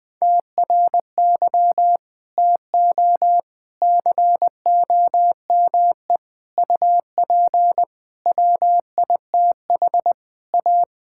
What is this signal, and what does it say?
Morse code, heard as a keyed tone, about 20 wpm, 700 Hz. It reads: TRY TO COME UP WITH A
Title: Sentences with 4-Word Phrases - Spoken Beginning - 20wpm